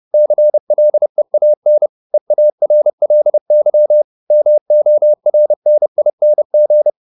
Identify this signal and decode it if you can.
Morse code, heard as a keyed tone, about 30 wpm, 600 Hz. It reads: CLEAN EARLY MORNING
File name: the old man and the sea / the old man and the sea 1 30wpm